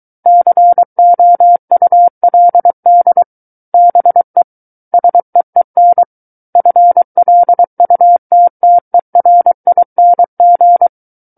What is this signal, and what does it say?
Morse code, heard as a keyed tone, about 23 wpm, 700 Hz. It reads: COULD BE SEEN FLUTTERING
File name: Book One / 14-Chapter 14 - 23 WPM